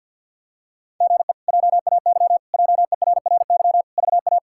Morse code, heard as a keyed tone, about 46 wpm, 700 Hz. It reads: GE JAY JERRY VA
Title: K1USN SST QSO Practice - 50wpm